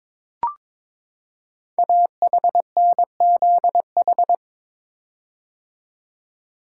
Morse code, heard as a keyed tone, about 22 wpm, 700 Hz. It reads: AHNZH
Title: Code Groups - 5 Letters - 22wpm